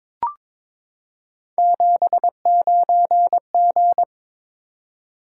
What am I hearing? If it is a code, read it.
Morse code, heard as a keyed tone, about 22 wpm, 700 Hz. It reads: 79G
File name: practice / 22wpm / Three Letter-Number Combinations - 22wpm